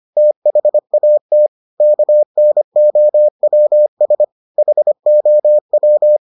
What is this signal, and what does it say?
Morse code, heard as a keyed tone, about 25 wpm, 600 Hz. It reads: THAT KNOWS HOW